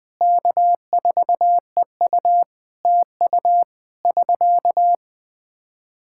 Morse code, heard as a keyed tone, about 20 wpm, 700 Hz. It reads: K4EU TU <SK>
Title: K1USN SST QSO Practice 20wpm